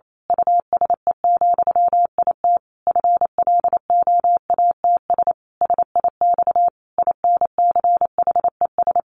Morse code, heard as a keyed tone, about 28 wpm, 700 Hz. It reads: USE, IT FLOATS SIX INCHES
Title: A Sound Of Thunder 28wpm